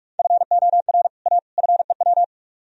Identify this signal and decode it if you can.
Morse code, heard as a keyed tone, about 44 wpm, 700 Hz. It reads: FOR A FEW